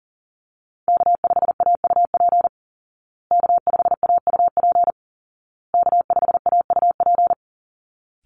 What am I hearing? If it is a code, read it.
Morse code, heard as a keyed tone, about 38 wpm, 700 Hz. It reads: K5AUP K5AUP K5AUP
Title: Call Signs Repeated 2x - 40wpm